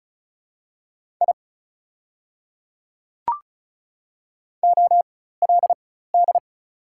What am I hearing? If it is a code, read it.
Morse code, heard as a keyed tone, about 36 wpm, spaced as for 16 wpm, 700 Hz. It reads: I OLD
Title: 4x Character Spacing - Top 300 Words - 35wpm